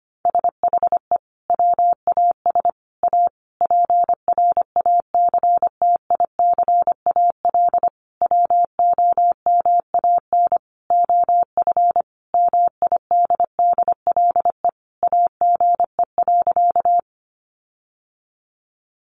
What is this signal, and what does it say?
Morse code, heard as a keyed tone, about 25 wpm, 700 Hz. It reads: SHE WAS A PRACTICAL WOMAN OF MIDDLE AGE.
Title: Sentences from Top 1400 Words 25wpm